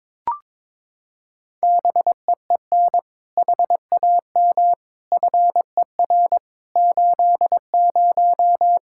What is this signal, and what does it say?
Morse code, heard as a keyed tone, about 22 wpm, 700 Hz. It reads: BEEN HAM FER 80
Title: K1USN SST QSO Practice 22wpm